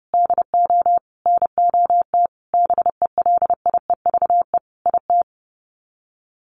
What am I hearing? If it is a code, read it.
Morse code, heard as a keyed tone, about 30 wpm, 700 Hz. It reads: DO NOT BELIEVE IT
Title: Sets of 5 Words 30wpm